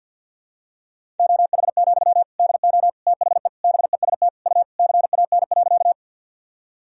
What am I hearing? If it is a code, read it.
Morse code, heard as a keyed tone, about 47 wpm, 700 Hz. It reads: OH, DO THE BEST U CAN.
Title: Sentences from Top 500 Words and Abbreviations 50wpm